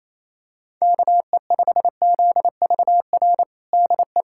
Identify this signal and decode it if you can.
Morse code, heard as a keyed tone, about 27 wpm, 700 Hz. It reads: KE5ZVR DE
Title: USA Callsign DE Callsign 28wpm